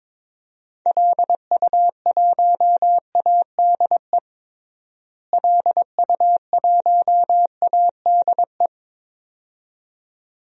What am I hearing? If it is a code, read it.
Morse code, heard as a keyed tone, about 21 wpm, 700 Hz. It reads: LU1ADE LU1ADE
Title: DXCC-100 Call Signs - 6 Characters - Repeated 2x - 22wpm